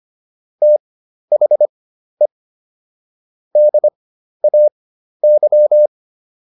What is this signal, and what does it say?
Morse code, heard as a keyed tone, about 25 wpm, spaced as for 12 wpm, 600 Hz. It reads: THE DAY